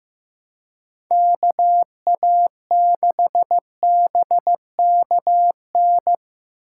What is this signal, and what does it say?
Morse code, heard as a keyed tone, about 15 wpm, 700 Hz. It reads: KA6BKN
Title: USA Call Signs - 2x Character Spacing - 15wpm